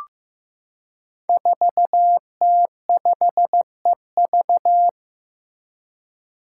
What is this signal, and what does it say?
Morse code, heard as a keyed tone, about 15 wpm, 700 Hz. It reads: 4T5EV
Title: Five Challenging Letter-Number Combinations 15wpm